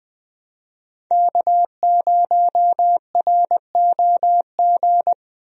Morse code, heard as a keyed tone, about 19 wpm, 700 Hz. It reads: K0ROG
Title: USA Call Signs - 5 Characters - Repeated 3x - 20wpm